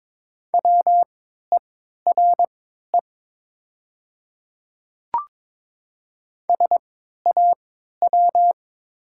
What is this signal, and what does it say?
Morse code, heard as a keyed tone, about 22 wpm, spaced as for 12 wpm, 700 Hz. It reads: WERE SAW
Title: 3x Character Spacing - Top 300 Words - 22wpm